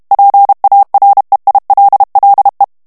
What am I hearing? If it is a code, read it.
Morse code, heard as a keyed tone, about 32 wpm, 800 Hz. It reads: PAREILLE
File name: FRANCAIS TEXTE CLAIR 2 VITESSE 30 MOTS MINUTES